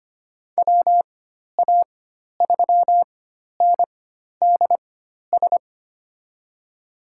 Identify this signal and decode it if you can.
Morse code, heard as a keyed tone, about 25 wpm, spaced as for 12 wpm, 700 Hz. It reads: WA3NDS
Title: USA Call Signs - 6 Characters - 4x Character Spacing - 25wpm